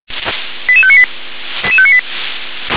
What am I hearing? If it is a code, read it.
Five-tone selective call (ZVEI), heard as quick runs of five tones, tones of about 70 ms. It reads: 88598